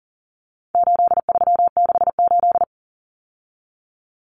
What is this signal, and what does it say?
Morse code, heard as a keyed tone, about 40 wpm, 700 Hz. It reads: 8368
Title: Numbers - 4-Digit 40wpm